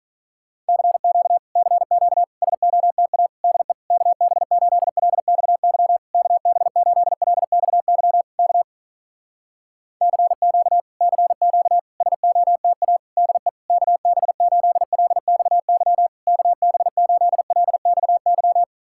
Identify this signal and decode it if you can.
Morse code, heard as a keyed tone, about 46 wpm, 700 Hz. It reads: CQ CQ SOTA DE KB8LXY KB8LXY K CQ CQ SOTA DE KB8LXY KB8LXY